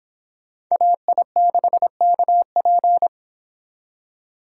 Morse code, heard as a keyed tone, about 28 wpm, 700 Hz. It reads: AI6KP